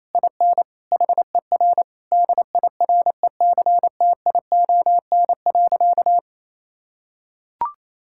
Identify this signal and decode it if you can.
Morse code, heard as a keyed tone, about 28 wpm, 700 Hz. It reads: IN HER DIRECTION.
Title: Sentences from Top 1000 Words - Review Part 5 - 28wpm